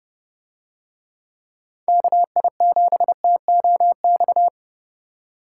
Morse code, heard as a keyed tone, about 29 wpm, 700 Hz. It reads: KI7TOX